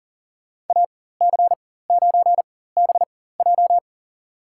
Morse code, heard as a keyed tone, about 40 wpm, spaced as for 19 wpm, 700 Hz. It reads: AC9BJ
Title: USA Call Signs - 5 Characters - 4x Character Spacing - 40wpm